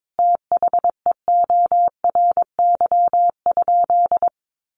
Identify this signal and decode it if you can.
Morse code, heard as a keyed tone, about 22 wpm, 700 Hz. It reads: THEORY?
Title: Sentences from Top 1000 Words 22wpm